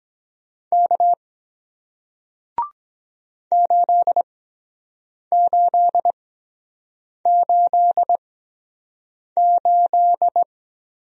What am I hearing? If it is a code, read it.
Morse code, heard as a keyed tone, about 23 wpm, spaced as for 7 wpm, 700 Hz. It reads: K 8888